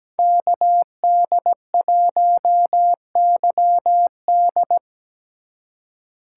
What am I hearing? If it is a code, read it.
Morse code, heard as a keyed tone, about 17 wpm, 700 Hz. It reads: KD1YD
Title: USA Call Signs - 5 Characters - 4x Character Spacing - 17wpm